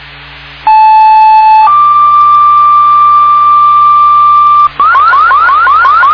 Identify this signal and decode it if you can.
Two-tone page, heard as a one-second tone, then a three-second tone, about 870 then 1180 Hz, with noise. It is heard from 0.7 to 4.7 s.